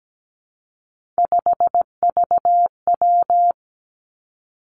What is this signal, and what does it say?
Morse code, heard as a keyed tone, about 17 wpm, 700 Hz. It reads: 5VW
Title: Three Challenging Letter-Number Combinations 17wpm